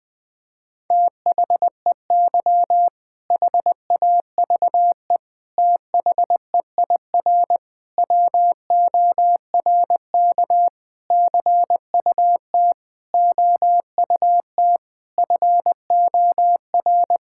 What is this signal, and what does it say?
Morse code, heard as a keyed tone, about 21 wpm, 700 Hz. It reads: THEY HAVE THEIR WORK CUT OUT FOR
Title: Sentences with 7-Word Phrases - 20wpm